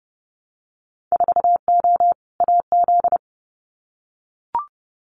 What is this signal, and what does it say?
Morse code, heard as a keyed tone, about 30 wpm, 700 Hz. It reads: 4O AZ
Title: Winter Field Day Exchanges Encore 30wpm